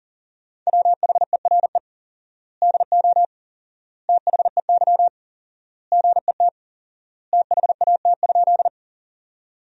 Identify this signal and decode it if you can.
Morse code, heard as a keyed tone, about 37 wpm, 700 Hz. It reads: WHERE DO THEY GET THAT?
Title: 4x Word-Spacing - Sentences from Top 200 Words - 40wpm